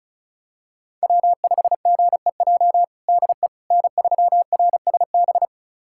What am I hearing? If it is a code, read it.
Morse code, heard as a keyed tone, about 36 wpm, 700 Hz. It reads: W5GEJ DE N3RSB
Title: USA Callsign DE Callsign - 5 Characters - 35wpm